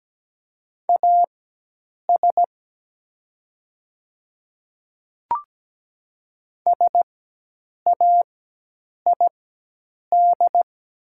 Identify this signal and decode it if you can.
Morse code, heard as a keyed tone, about 17 wpm, spaced as for 8 wpm, 700 Hz. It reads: AS SAID
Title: 4x Character Spacing - Top 300 Words - 17wpm